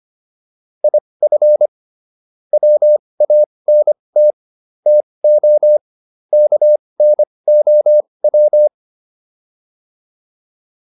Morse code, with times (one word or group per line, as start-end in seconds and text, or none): 0.8-1.7 IF
2.5-4.3 WANT
4.9-5.8 TO
6.3-8.7 KNOW